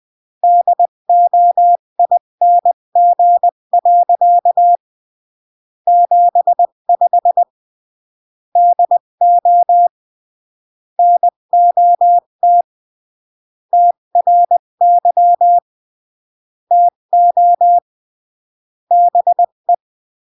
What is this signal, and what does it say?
Morse code, heard as a keyed tone, about 20 wpm, 700 Hz. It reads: DOING. 75 DO NOT TRY TO BE
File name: CW / jodorowsky / 82 commandments 20-15